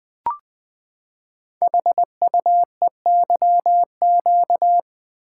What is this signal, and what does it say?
Morse code, heard as a keyed tone, about 20 wpm, 700 Hz. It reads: HUEYQ
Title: Code Groups - 5 Letters - 20wpm